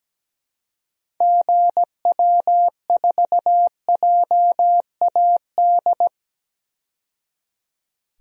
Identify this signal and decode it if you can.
Morse code, heard as a keyed tone, about 17 wpm, 700 Hz. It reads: GW4JAD
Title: DXCC-100 Call Signs - 6 Characters - 17wpm